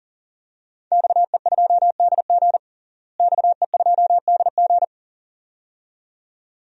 Morse code, heard as a keyed tone, about 37 wpm, 700 Hz. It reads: XE2DG XE2DG
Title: DXCC-100 Call Signs - 5 Characters - Repeated 2x - 40wpm